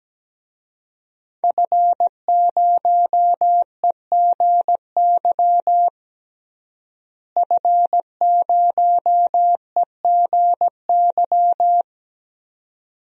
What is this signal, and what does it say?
Morse code, heard as a keyed tone, about 17 wpm, 700 Hz. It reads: F0EGY F0EGY
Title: DXCC-100 Call Signs - 5 Characters - Repeated 2x - 17wpm